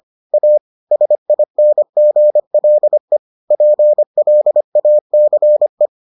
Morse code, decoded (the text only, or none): A SINGLE PLACE